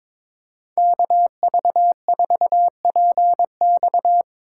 Morse code, heard as a keyed tone, about 24 wpm, 700 Hz. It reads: KV4PX
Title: Speed Racing - USA 2x2 Call Signs 22wpm